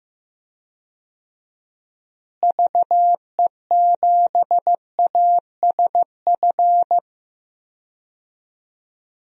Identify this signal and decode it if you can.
Morse code, heard as a keyed tone, about 16 wpm, 700 Hz. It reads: VE7ASF
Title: DXCC-100 Call Signs - 6 Characters - Repeated 3x - 15wpm